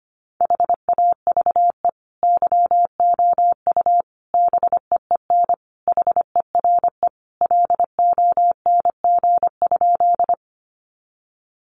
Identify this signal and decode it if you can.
Morse code, heard as a keyed tone, about 25 wpm, 700 Hz. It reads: HAVE YOU BEEN HERE LONG?
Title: Sentences from Top 200 Words 25wpm